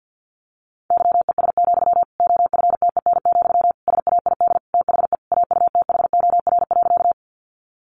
Morse code, heard as a keyed tone, about 50 wpm, 700 Hz. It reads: YES, OFTEN, SAID THE AUTHOR.
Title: Sentences from Top 1200 Words 50wpm